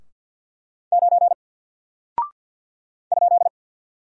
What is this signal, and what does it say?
Morse code, heard as a keyed tone, about 50 wpm, 700 Hz. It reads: 9 ?
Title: Single Character (Letter:Number:Punctuation:Prosign) 50wpm